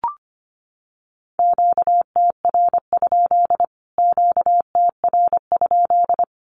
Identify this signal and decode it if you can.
Morse code, heard as a keyed tone, about 25 wpm, 700 Hz. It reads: QTR? QTR?
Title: Q-Codes Repeated 3x 25wpm